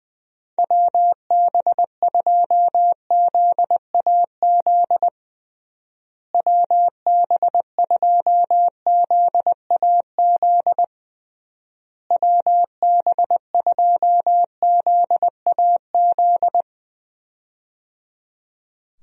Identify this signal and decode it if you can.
Morse code, heard as a keyed tone, about 20 wpm, 700 Hz. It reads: WB2ZAZ WB2ZAZ WB2ZAZ
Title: USA Call Signs - 6 Characters - Repeated 3x - 20wpm